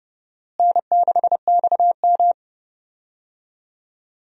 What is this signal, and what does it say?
Morse code, heard as a keyed tone, about 31 wpm, 700 Hz. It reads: N6XM
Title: USA Call Signs - 4 Characters - Repeated 2x - 30wpm